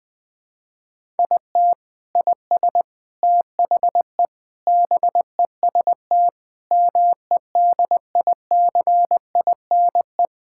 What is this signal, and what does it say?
Morse code, heard as a keyed tone, about 20 wpm, 700 Hz. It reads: IT IS THE BEST MEDICINE
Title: Sentences from Top 1000 Words - Review Part 6 - 20wpm